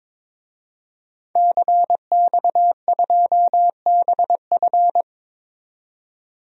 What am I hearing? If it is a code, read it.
Morse code, heard as a keyed tone, about 22 wpm, 700 Hz. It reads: CX2BF
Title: DXCC-100 Call Signs - 5 Characters - 22wpm